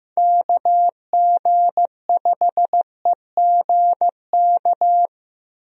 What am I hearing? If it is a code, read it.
Morse code, heard as a keyed tone, about 15 wpm, 700 Hz. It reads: KG5EGK
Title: USA Call Signs - 6 Characters - Repeated 2x - 15wpm